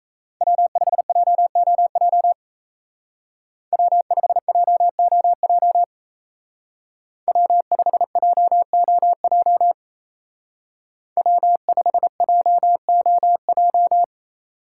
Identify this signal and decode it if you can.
Morse code, heard as a keyed tone, about 34 wpm, 700 Hz. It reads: W5JOJ W5JOJ W5JOJ W5JOJ
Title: Speed Racing - USA 1x3 Call Signs 28wpm